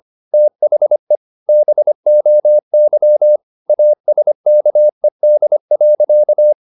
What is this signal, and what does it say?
Morse code, heard as a keyed tone, about 25 wpm, 600 Hz. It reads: THE BOY ASKED.
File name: the old man and the sea part 1 25wpm